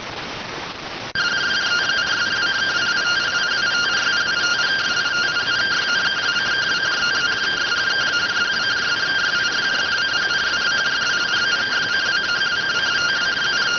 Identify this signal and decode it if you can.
Two-tone teleprinter signal, about 170 Hz shift.